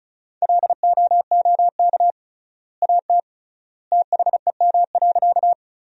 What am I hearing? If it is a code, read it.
Morse code, heard as a keyed tone, about 36 wpm, 700 Hz. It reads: LOOK AT THEM.
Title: 3x Word-Spacing - Sentences from Top 100 Words - 35wpm